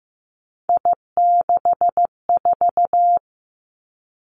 Morse code, heard as a keyed tone, about 15 wpm, 700 Hz. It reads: I64
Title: Three Challenging Letter-Number Combinations 15wpm